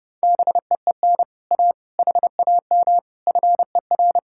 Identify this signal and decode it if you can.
Morse code, heard as a keyed tone, about 30 wpm, 700 Hz. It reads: BEEN A HAM FER